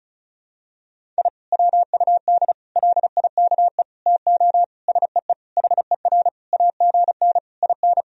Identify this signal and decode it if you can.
Morse code, heard as a keyed tone, about 34 wpm, 700 Hz. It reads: I WUD LIKE TO SEE HER AGN IN
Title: Sentences from Top 200 Words and Abbreviations 35wpm